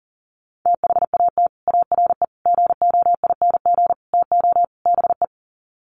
Morse code, heard as a keyed tone, about 40 wpm, 700 Hz. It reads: THAT ARE GOING TO BE
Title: Sets of 5 Words Encore2 40wpm